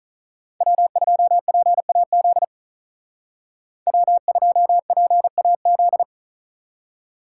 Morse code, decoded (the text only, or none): W2PAZ W2PAZ